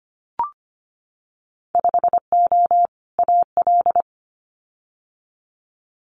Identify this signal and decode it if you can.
Morse code, heard as a keyed tone, about 25 wpm, 700 Hz. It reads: 5O AL
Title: Winter Field Day Exchanges 25wpm